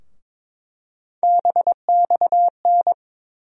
Morse code, heard as a keyed tone, about 22 wpm, 700 Hz. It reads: BXN
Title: USA Call Sign Postfixes - 3 Letters 22wpm